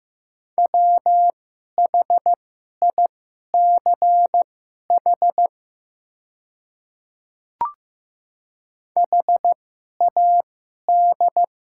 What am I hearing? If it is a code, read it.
Morse code, heard as a keyed tone, about 15 wpm, spaced as for 11 wpm, 700 Hz. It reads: WHICH HAD
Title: 2x Character Spacing - Top 300 Words - 15wpm